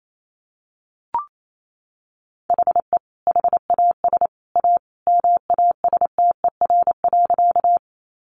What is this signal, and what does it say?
Morse code, heard as a keyed tone, about 28 wpm, 700 Hz. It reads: HE HAS A MASTER.
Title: Sentences from Top 1300 Words 28wpm